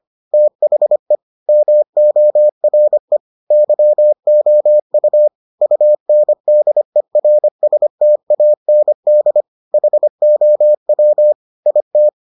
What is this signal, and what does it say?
Morse code, heard as a keyed tone, about 25 wpm, 600 Hz. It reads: THE MORE YOU UNDERSTAND HOW IT